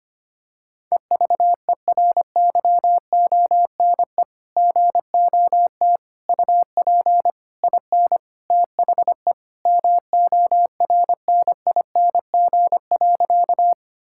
Morse code, heard as a keyed tone, about 25 wpm, 700 Hz. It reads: EVERYONE GOT UP IN THE MORNING.
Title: Sentences from Top 1000 Words - Review Part 5 - 25wpm